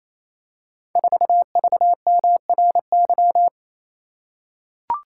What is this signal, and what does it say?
Morse code, heard as a keyed tone, about 27 wpm, 700 Hz. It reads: 4VMRY
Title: Code Groups - 5 Letter:Numbers - 28wpm